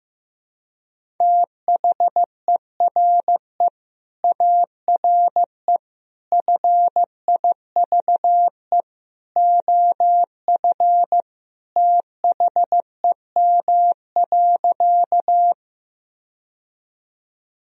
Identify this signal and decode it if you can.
Morse code, heard as a keyed tone, about 16 wpm, 700 Hz. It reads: THERE ARE FIVE OF THEM.
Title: Sentences from Top 1000 Words - Review Part 2 - 15wpm